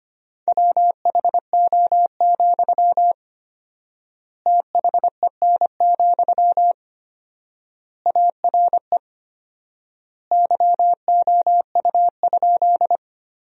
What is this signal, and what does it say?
Morse code, heard as a keyed tone, about 25 wpm, 700 Hz. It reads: WHO, THEN, ARE YOU?
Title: 4x Word-Spacing - Sentences from Top 100 Words - 25wpm